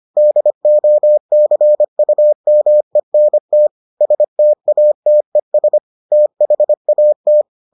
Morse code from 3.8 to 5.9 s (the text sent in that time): STATES